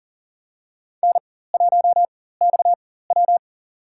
Morse code, heard as a keyed tone, about 42 wpm, spaced as for 19 wpm, 700 Hz. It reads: N1XW
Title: USA Call Signs - 4 Characters - 4x Character Spacing - 40wpm